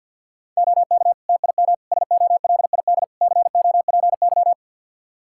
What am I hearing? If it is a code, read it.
Morse code, heard as a keyed tone, about 51 wpm, 700 Hz. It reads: OK TIM SOLID COPY
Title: Practice QSO - Exchange 3 - More Info - 50wpm